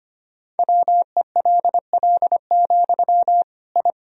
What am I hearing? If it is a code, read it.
Morse code, heard as a keyed tone, about 25 wpm, 700 Hz. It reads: WELL, I
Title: Sentences from Top 1000 Words - Review Part 2 - 25wpm